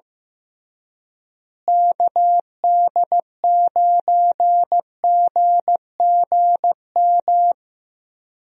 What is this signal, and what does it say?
Morse code, heard as a keyed tone, about 15 wpm, 700 Hz. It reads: KD9GGM
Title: USA Call Signs - 6 Characters - Repeated 3x - 15wpm